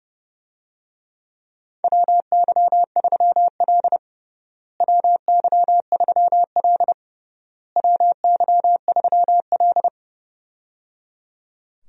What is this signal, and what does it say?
Morse code, heard as a keyed tone, about 30 wpm, 700 Hz. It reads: WY3L WY3L WY3L
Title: USA Call Signs - 4 Characters - Repeated 3x - 30wpm